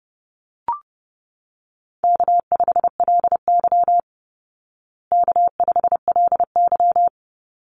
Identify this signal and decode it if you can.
Morse code, heard as a keyed tone, about 28 wpm, 700 Hz. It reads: K5LY K5LY
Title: Easy Call Signs Repeated 2x - 30wpm